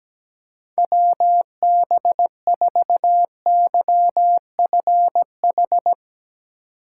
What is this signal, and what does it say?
Morse code, heard as a keyed tone, about 17 wpm, 700 Hz. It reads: WB4YFH
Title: USA Call Signs - 2x Character Spacing - 17wpm